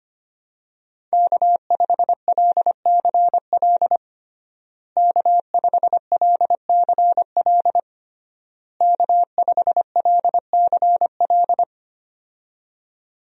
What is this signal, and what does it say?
Morse code, heard as a keyed tone, about 25 wpm, 700 Hz. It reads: K5LCL K5LCL K5LCL
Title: USA Call Signs - 5 Characters - Repeated 3x - 25wpm